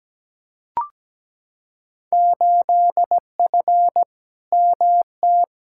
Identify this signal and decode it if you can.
Morse code, heard as a keyed tone, about 17 wpm, 700 Hz. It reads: 8F MT
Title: Field Day Exchanges Encore - 17wpm